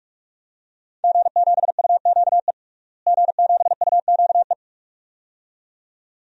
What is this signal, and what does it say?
Morse code, heard as a keyed tone, about 41 wpm, 700 Hz. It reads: G7UQE G7UQE
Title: DXCC-100 Call Signs - 5 Characters - Repeated 2x - 45wpm